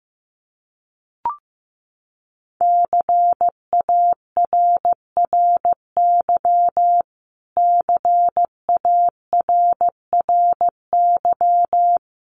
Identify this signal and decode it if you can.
Morse code, heard as a keyed tone, about 15 wpm, 700 Hz. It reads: CARRY CARRY
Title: Top 500 Words Repeated 15wpm